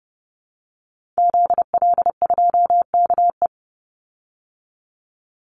Morse code, heard as a keyed tone, about 30 wpm, 700 Hz. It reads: ZL2KE
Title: Call Signs Repeated 2x - 30wpm